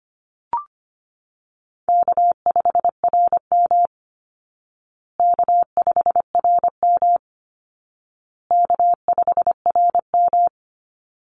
Morse code, heard as a keyed tone, about 25 wpm, 700 Hz. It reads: K5RM K5RM K5RM
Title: Easy Call Signs Repeated 3x - 25wpm